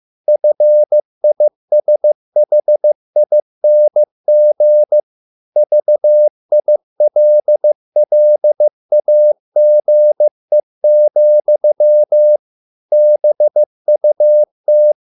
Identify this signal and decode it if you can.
Morse code, heard as a keyed tone, about 15 wpm, 600 Hz. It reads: FISHING VILLAGE, BUT